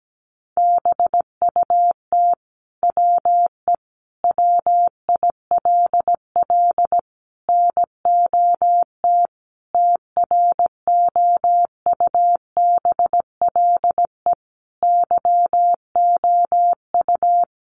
Morse code, heard as a keyed tone, about 17 wpm, 700 Hz. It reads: BUT WE WILL NOT TROUBLE YOU
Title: Sentences from Top 1200 Words 17wpm